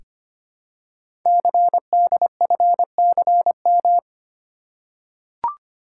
Morse code, heard as a keyed tone, about 26 wpm, 700 Hz. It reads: CDFCM
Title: Code Groups - 5 Letters - 25wpm